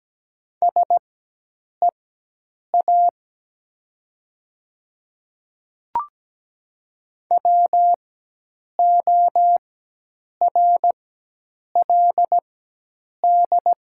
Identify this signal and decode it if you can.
Morse code, heard as a keyed tone, about 17 wpm, spaced as for 8 wpm, 700 Hz. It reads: SEA WORLD
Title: 4x Character Spacing - Top 300 Words - 17wpm